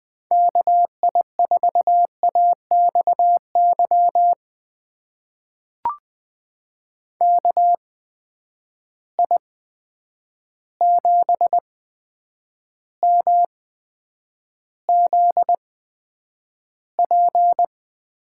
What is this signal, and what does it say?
Morse code, heard as a keyed tone, about 21 wpm, spaced as for 9 wpm, 700 Hz. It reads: KI4AXY KI7MZP
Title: USA Call Signs - 8x Character Spacing - 20wpm